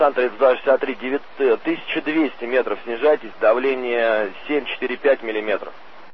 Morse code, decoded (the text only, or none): none